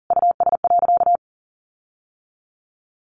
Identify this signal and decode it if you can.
Morse code, heard as a keyed tone, about 40 wpm, 700 Hz. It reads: US.
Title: Sentences from Top 1300 Words 40wpm